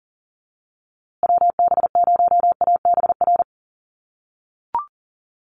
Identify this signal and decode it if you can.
Morse code, heard as a keyed tone, about 37 wpm, 700 Hz. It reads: WB0ABR
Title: USA Call Signs Repeated 3x 40wpm